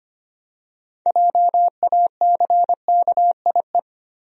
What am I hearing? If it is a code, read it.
Morse code, heard as a keyed tone, about 25 wpm, 700 Hz. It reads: JACKIE
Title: USA Common Names - 25wpm